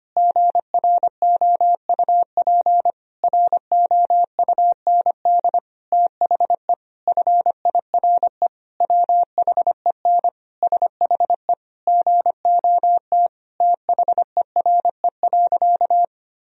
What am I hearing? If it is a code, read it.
Morse code, heard as a keyed tone, about 25 wpm, 700 Hz. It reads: GROUP ROUND THE FIRE WHEN SHE GOT THERE.
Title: Sentences from Top 1000 Words - Review Part 1 - 25wpm